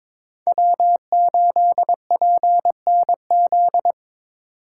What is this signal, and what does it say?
Morse code, heard as a keyed tone, about 21 wpm, 700 Hz. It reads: W8PNZ
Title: USA Call Signs - 5 Characters - 4x Character Spacing - 22wpm